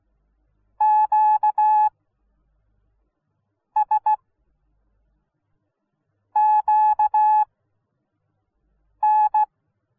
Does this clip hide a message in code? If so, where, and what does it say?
Yes: Morse code from 0.8 to 9.4 s, heard as a keyed tone, about 16 wpm, spaced as for 5 wpm, 850 Hz. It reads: QS QN